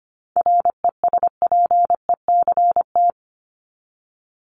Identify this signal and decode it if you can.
Morse code, heard as a keyed tone, about 25 wpm, 700 Hz. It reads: RESPECT
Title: Top 1501-1750 Words 25wpm